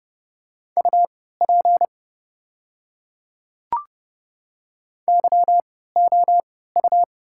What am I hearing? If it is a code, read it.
Morse code, heard as a keyed tone, about 28 wpm, spaced as for 17 wpm, 690 Hz. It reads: UP YOU
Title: 3x Character Spacing - Top 100 Words - 30wpm